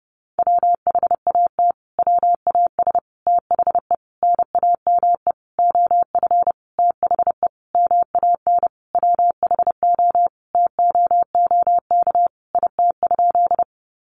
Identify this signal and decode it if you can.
Morse code, heard as a keyed tone, about 30 wpm, 700 Hz. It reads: WHAT WAS THE NAME OF THE MAN WHO TOOK IT?
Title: Sentences from Top 300 Words 30wpm